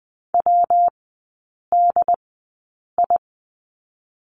Morse code, heard as a keyed tone, about 20 wpm, 700 Hz. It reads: W D I
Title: 114 Lesson 19e - Letter w - Three Characters 20wpm